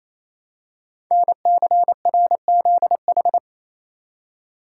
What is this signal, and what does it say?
Morse code, heard as a keyed tone, about 27 wpm, 700 Hz. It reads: NCRZH
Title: Code Groups - 5 Letters - 28wpm